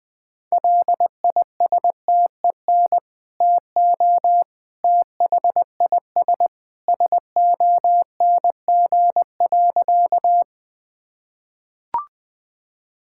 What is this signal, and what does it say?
Morse code, heard as a keyed tone, about 19 wpm, 700 Hz. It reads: LISTEN TO THIS SONG.
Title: Sentences with 2-Word Phrases - Spoken Beginning - 20wpm